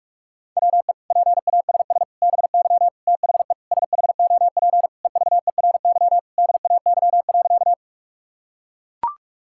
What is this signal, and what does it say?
Morse code, heard as a keyed tone, about 42 wpm, 700 Hz. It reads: WE PASS BY THE SHOP EVERY DAY.
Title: Sentences from Top 1600 Words 45wpm